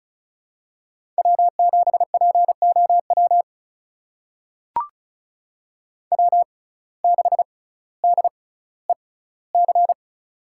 Morse code, heard as a keyed tone, about 36 wpm, spaced as for 18 wpm, 700 Hz. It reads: W7POW W6DEC
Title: USA Call Signs - 5 Characters - 6x Character Spacing - 35wpm